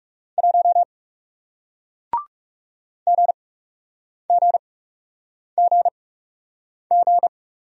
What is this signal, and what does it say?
Morse code, heard as a keyed tone, about 39 wpm, spaced as for 8 wpm, 700 Hz. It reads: 1 GGGG